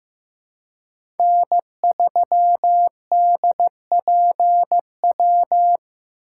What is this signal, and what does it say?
Morse code, heard as a keyed tone, about 18 wpm, 700 Hz. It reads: N3DPW
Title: Speed Racing - USA 1x3 Call Signs 15wpm